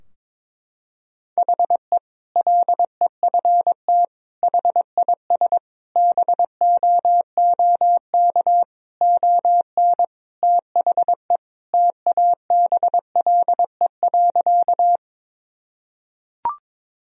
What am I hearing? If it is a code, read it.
Morse code, heard as a keyed tone, about 22 wpm, 700 Hz. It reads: HE LEFT HIS BOOK ON THE TABLE.
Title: Sentences from Top 1000 Words - Review Part 2 - 22wpm